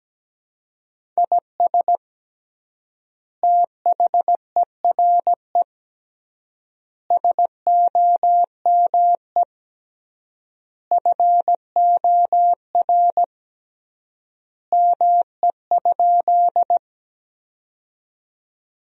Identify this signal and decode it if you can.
Morse code, heard as a keyed tone, about 17 wpm, 700 Hz. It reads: IS THERE SOME FOR ME?
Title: 3x Word-Spacing - Sentences from Top 200 Words - 17wpm